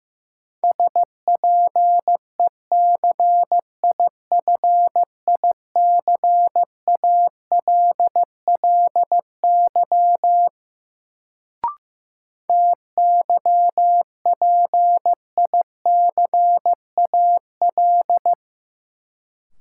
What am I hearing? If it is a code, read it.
Morse code, heard as a keyed tone, about 15 wpm, 700 Hz. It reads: SPECIFICALLY TYPICAL
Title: Sentences from Top 1100 Words - New Word Drill - 15wpm